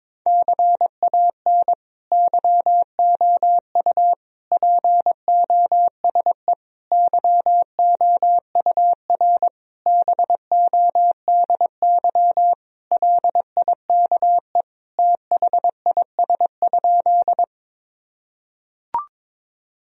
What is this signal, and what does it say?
Morse code, heard as a keyed tone, about 22 wpm, 700 Hz. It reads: CAN YOU POSE YOUR BODY LIKE THIS?
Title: Sentences from Top 1000 Words - Review Part 2 - 22wpm